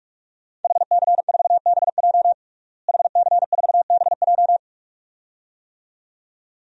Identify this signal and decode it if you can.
Morse code, heard as a keyed tone, about 46 wpm, 700 Hz. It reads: HC4BJ HC4BJ
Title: DXCC-100 Call Signs - 5 Characters - Repeated 2x - 45wpm